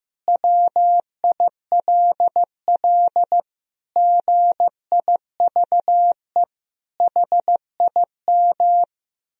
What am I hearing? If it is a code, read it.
Morse code, heard as a keyed tone, about 15 wpm, 700 Hz. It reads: WILL GIVE HIM
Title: Sentences from Top 1000 Words - Review Part 4 - 15wpm